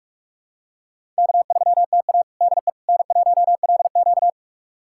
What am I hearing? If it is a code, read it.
Morse code, heard as a keyed tone, about 41 wpm, 700 Hz. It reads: K3TA DE N1LQ
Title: USA Callsign DE Callsign - 4 Characters - 45wpm